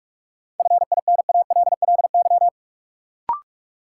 Morse code, heard as a keyed tone, about 46 wpm, 700 Hz. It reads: FINALLY
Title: Sentences from Top 1000 Words - New Word Drill - 45wpm